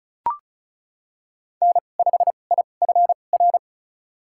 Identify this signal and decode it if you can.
Morse code, heard as a keyed tone, about 36 wpm, spaced as for 25 wpm, 700 Hz. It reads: N5IFR
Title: USA Call Signs - 2x Character Spacing - 35wpm